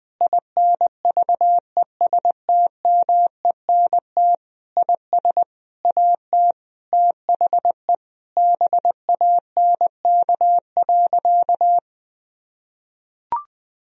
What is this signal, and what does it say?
Morse code, heard as a keyed tone, about 21 wpm, 700 Hz. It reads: INVESTMENT IS AT THE BANK.
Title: Sentences from Top 1000 Words - Review Part 4 - 20wpm